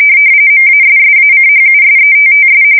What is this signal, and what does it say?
RTTY, about 45.45 baud. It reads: MASSACHUSETTS
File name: fsk170 / USS-Massachusetts